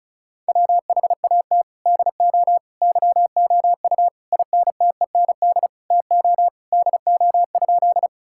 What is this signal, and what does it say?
Morse code, heard as a keyed tone, about 36 wpm, 700 Hz. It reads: WHAT DO YOU INTEND TO DO?
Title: Sentences from Top 1600 Words 35wpm